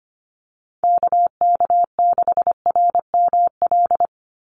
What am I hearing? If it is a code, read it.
Morse code, heard as a keyed tone, about 25 wpm, 700 Hz. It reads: KK6RML
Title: USA Call Signs 25wpm